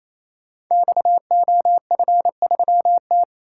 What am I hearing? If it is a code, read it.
Morse code, heard as a keyed tone, about 29 wpm, 700 Hz. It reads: XOF3T
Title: Code Groups - 5 Letter:Numbers - 28wpm